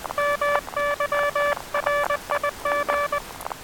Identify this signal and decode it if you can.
Morse code, heard as a keyed tone, about 20 wpm, 580 Hz, with noise. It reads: MY RIG
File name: attachments / 23.04.2017 DSB TRX OM3JA